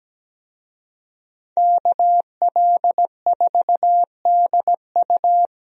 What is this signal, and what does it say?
Morse code, heard as a keyed tone, about 17 wpm, 700 Hz. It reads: KL4DU
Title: USA Call Signs - 5 Characters - Repeated 3x - 17wpm